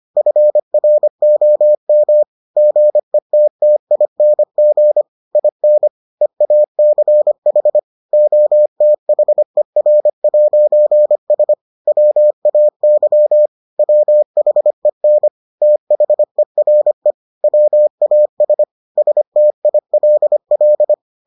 Morse code, heard as a keyed tone, about 25 wpm, 600 Hz. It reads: FROM GETTING IN EACH OTHER'S WAY WHEN THERE WAS STILL